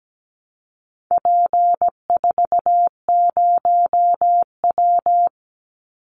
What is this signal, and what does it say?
Morse code, heard as a keyed tone, about 17 wpm, 700 Hz. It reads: P40W
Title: Call Signs Repeated 3x - 17wpm